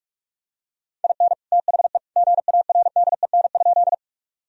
Morse code, heard as a keyed tone, about 43 wpm, 700 Hz. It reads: IN THE GARDEN?
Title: Sentences from Top 1000 Words - Review Part 4 - 45wpm